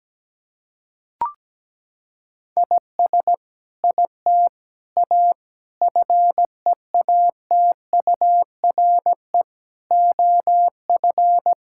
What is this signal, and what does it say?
Morse code, heard as a keyed tone, about 17 wpm, 700 Hz. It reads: IS IT A FEATURE OF
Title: Sentences from Top 1000 Words - Review Part 5 - 17wpm